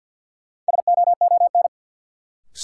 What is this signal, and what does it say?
Morse code, heard as a keyed tone, about 48 wpm, 700 Hz. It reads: SOON
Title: Sentences from Top 400 Words - New Word Drill - 50wpm